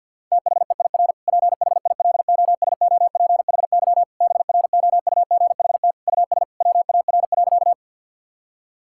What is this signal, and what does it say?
Morse code, heard as a keyed tone, about 50 wpm, 700 Hz. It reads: THEIR PHILOSOPHY BROUGHT US WAR.
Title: Sentences from Top 1000 Words - Review Part 6 - 50wpm